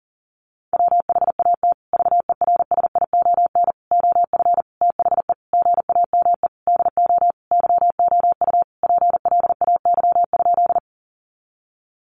Morse code, heard as a keyed tone, about 37 wpm, 700 Hz. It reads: WHAT VERSION OF THE GAME DO YOU PLAY?
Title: Sentences from Top 1400 Words 40wpm